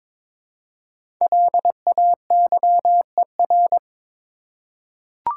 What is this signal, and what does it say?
Morse code, heard as a keyed tone, about 22 wpm, 700 Hz. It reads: LAYER
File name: 5 Letter Words - First 3 Letters Given - 22wpm